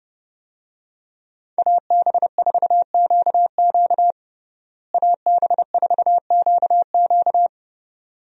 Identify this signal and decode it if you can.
Morse code, heard as a keyed tone, about 32 wpm, 700 Hz. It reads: AB4QQ AB4QQ
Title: USA Call Signs - 5 Characters - Repeated 2x - 30wpm